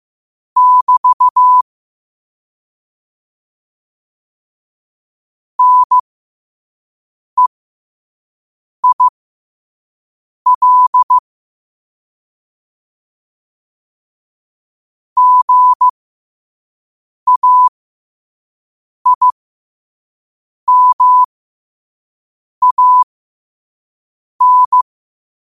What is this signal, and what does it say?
Morse code, heard as a keyed tone, about 15 wpm, spaced as for 5 wpm, 1000 Hz. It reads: <BT> NEIL GAIMAN
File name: qotdfiles / 5wpm